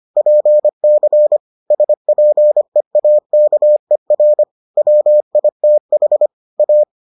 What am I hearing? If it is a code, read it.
Morse code, heard as a keyed tone, about 25 wpm, 600 Hz. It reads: PC SPEAKER WITH A